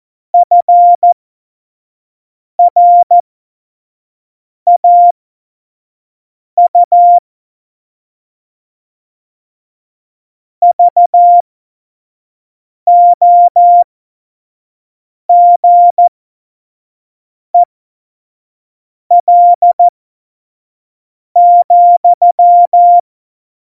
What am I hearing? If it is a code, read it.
Morse code, heard as a keyed tone, about 14 wpm, spaced as for 5 wpm, 700 Hz. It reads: FRAU VOGEL,
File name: Morsekurs / Witze / Witz 08. September 2020, 25 F70